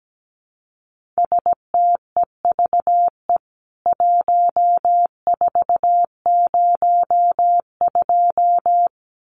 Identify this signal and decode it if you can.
Morse code, heard as a keyed tone, about 17 wpm, 700 Hz. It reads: STEVE 1402
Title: CWT Exchange 17wpm